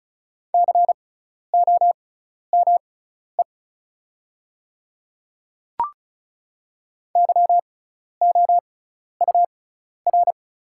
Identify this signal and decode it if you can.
Morse code, heard as a keyed tone, about 36 wpm, spaced as for 12 wpm, 690 Hz. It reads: COME YOUR
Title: 6x Character Spacing - Top 300 Words - 35wpm